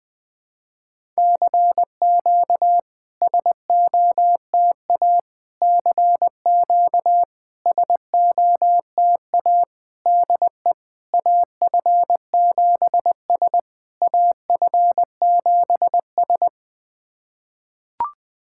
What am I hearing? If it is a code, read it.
Morse code, heard as a keyed tone, about 21 wpm, 700 Hz. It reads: CQ SOTA CQ SOTA DE AF7S AF7S
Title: SOTA QSO Practice 20wpm